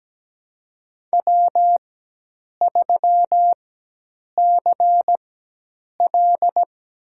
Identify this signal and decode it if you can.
Morse code, heard as a keyed tone, about 17 wpm, spaced as for 8 wpm, 700 Hz. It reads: W3CL
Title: USA Call Signs - 4 Characters - 4x Character Spacing - 17wpm